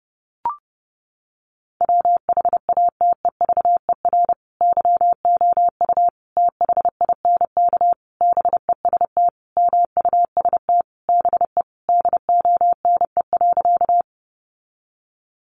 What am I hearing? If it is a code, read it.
Morse code, heard as a keyed tone, about 30 wpm, 700 Hz. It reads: WHATEVER YOU THINK BEST MUST BE DONE.
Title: Sentences from Top 1100 Words 30wpm